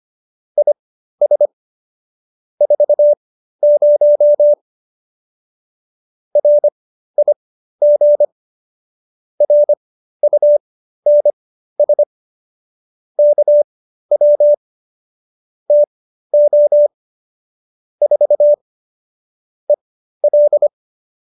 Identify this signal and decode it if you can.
Morse code, heard as a keyed tone, about 25 wpm, spaced as for 13 wpm, 590 Hz. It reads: IS 40 RIG RUNS KW TO 4 EL